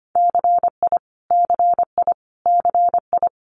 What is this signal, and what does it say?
Morse code, heard as a keyed tone, about 25 wpm, 700 Hz. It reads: CI CI CI
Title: Multi-letter Phonemes Repeated 3x 25wpm